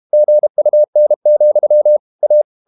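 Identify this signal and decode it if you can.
Morse code, heard as a keyed tone, about 32 wpm, 600 Hz. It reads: GUN, A
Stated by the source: Morse code practice at 32 wpm from CW med Gnister, reading Ken Follett's novel Never in English.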